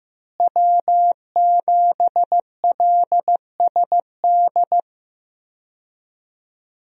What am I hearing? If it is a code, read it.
Morse code, heard as a keyed tone, about 15 wpm, 700 Hz. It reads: W7LSD
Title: USA Call Signs - 5 Characters - Repeated 3x - 15wpm